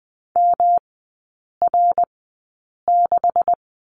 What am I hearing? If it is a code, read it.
Morse code, heard as a keyed tone, about 20 wpm, spaced as for 8 wpm, 700 Hz. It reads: MR6
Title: 159 Lesson 26e - Letter g - Three Characters 20wpm